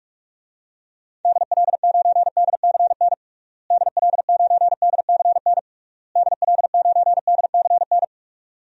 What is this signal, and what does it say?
Morse code, heard as a keyed tone, about 46 wpm, 700 Hz. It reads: DL9DCN DL9DCN DL9DCN
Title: DXCC-100 Call Signs - 6 Characters - Repeated 3x - 45wpm